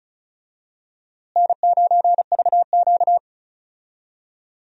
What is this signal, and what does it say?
Morse code, heard as a keyed tone, about 34 wpm, 700 Hz. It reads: N9VQ
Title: USA Call Signs - 4 Characters - Repeated 2x - 35wpm